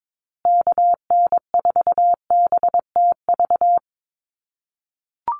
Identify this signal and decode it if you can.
Morse code, heard as a keyed tone, about 22 wpm, 700 Hz. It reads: KN4BTV
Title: USA Call Signs Repeated 2x 22wpm